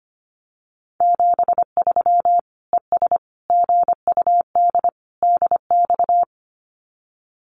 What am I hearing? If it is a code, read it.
Morse code, heard as a keyed tone, about 25 wpm, 700 Hz. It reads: 73 ES GUD DX
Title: Common QSO Phrases 25wpm